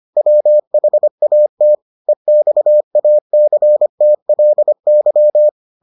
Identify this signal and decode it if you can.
Morse code, heard as a keyed tone, about 25 wpm, 600 Hz. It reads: WHAT EXACTLY